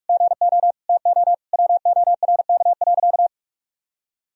Morse code, heard as a keyed tone, about 46 wpm, 700 Hz. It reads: GO TO WORK.
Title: Sentences from Top 1000 Words - Review Part 2 - 45wpm